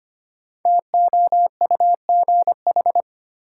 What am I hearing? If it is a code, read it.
Morse code, heard as a keyed tone, about 25 wpm, 700 Hz. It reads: TOUGH
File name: 5 Letter Words - First 3 Letters Given - 25wpm